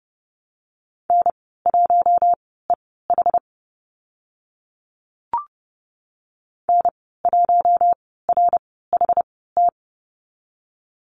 Morse code, heard as a keyed tone, about 30 wpm, spaced as for 17 wpm, 700 Hz. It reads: N1EH N1RHT
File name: Lesson 14g - Number 2 - Call Signs 30wpm